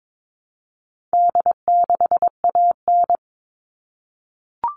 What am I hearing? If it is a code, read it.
Morse code, heard as a keyed tone, about 22 wpm, 700 Hz. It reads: D6AN
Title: All Cut Numbers - 4-Digit 22wpm